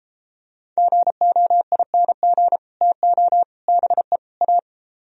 Morse code, heard as a keyed tone, about 33 wpm, 700 Hz. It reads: GOING TO BE A